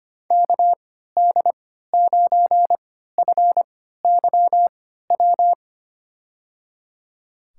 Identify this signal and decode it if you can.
Morse code, heard as a keyed tone, about 25 wpm, spaced as for 14 wpm, 700 Hz. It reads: KD9FYW
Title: USA Call Signs - 6 Characters - 3x Character Spacing - 25wpm